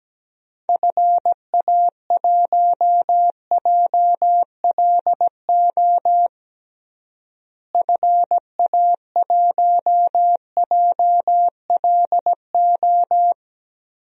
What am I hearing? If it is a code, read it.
Morse code, heard as a keyed tone, about 17 wpm, 700 Hz. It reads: FA1JLO FA1JLO
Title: DXCC-100 Call Signs - 6 Characters - Repeated 2x - 17wpm